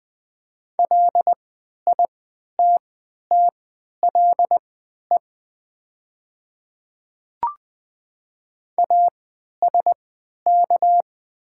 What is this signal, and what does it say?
Morse code, heard as a keyed tone, about 20 wpm, spaced as for 11 wpm, 700 Hz. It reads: LITTLE ASK
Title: 3x Character Spacing - Top 200 Words - 20wpm